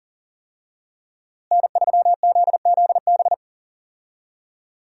Morse code, heard as a keyed tone, about 40 wpm, 700 Hz. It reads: N3ZZB
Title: USA Call Signs - 5 Characters - Repeated 3x - 40wpm